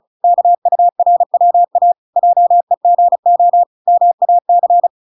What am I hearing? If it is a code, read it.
Morse code, heard as a keyed tone, about 34 wpm, 700 Hz. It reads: KURWA JEGO MAC